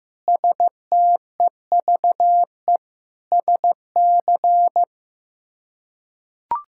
Morse code, heard as a keyed tone, about 15 wpm, 700 Hz. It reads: STEVE SC
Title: K1USN SST QSO Practice - 15wpm